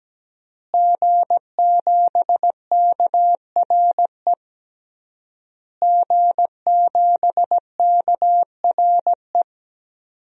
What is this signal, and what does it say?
Morse code, heard as a keyed tone, about 17 wpm, 700 Hz. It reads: G7KRE G7KRE
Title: DXCC-100 Call Signs - 5 Characters - Repeated 3x - 17wpm